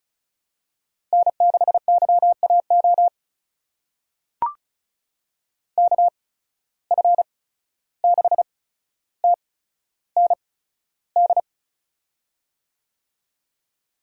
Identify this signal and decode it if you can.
Morse code, heard as a keyed tone, about 36 wpm, spaced as for 14 wpm, 700 Hz. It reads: N6YAO KF6TND
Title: USA Call Signs - 8x Character Spacing - 35wpm